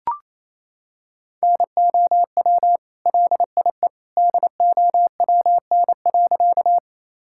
Morse code, heard as a keyed tone, about 28 wpm, 700 Hz. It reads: NOW LIE DOWN.
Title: Sentences from Top 1600 Words 28wpm